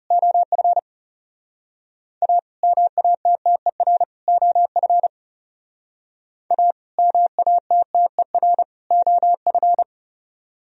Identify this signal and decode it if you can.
Morse code, heard as a keyed tone, about 39 wpm, 700 Hz. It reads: OF A MATTER OF A MATTER OF